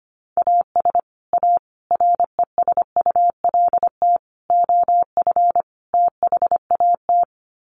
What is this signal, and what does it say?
Morse code, heard as a keyed tone, about 25 wpm, 700 Hz. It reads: AS A RESULT OF THAT
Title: Sets of 5 Words Encore 25wpm